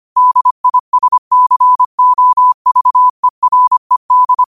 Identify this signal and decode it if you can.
Morse code, heard as a keyed tone, about 25 wpm, 1000 Hz. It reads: DISCOVERED